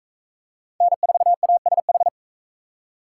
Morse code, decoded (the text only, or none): N4ASH